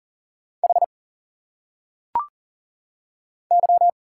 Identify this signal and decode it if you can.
Morse code, heard as a keyed tone, about 40 wpm, 700 Hz. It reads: H Y
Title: Single Letters - Phonetic Pronunciation 40wpm